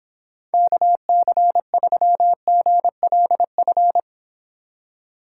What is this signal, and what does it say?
Morse code, heard as a keyed tone, about 26 wpm, 700 Hz. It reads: KC3GLF